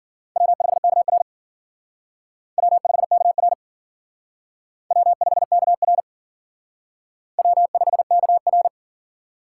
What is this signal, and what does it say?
Morse code, heard as a keyed tone, about 50 wpm, 700 Hz. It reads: W5KR W5KR W5KR W5KR
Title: Speed Racing - USA 1x2 Call Signs 40wpm